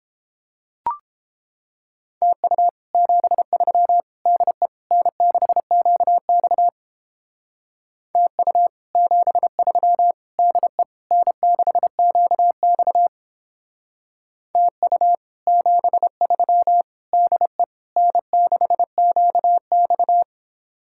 Morse code, heard as a keyed tone, about 29 wpm, 700 Hz. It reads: TU 73 DE N6QX TU 73 DE N6QX TU 73 DE N6QX